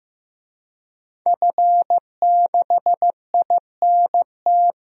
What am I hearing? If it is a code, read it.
Morse code, heard as a keyed tone, about 15 wpm, 700 Hz. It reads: F6INT
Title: DXCC-100 Call Signs - 5 Characters - 15wpm